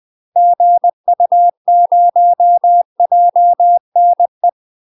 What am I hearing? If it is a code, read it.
Morse code, heard as a keyed tone, about 20 wpm, 700 Hz. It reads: GU0JNE
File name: UK MASTER QSOs TRAINING / 20WPM / UK MASTER QSOs 0023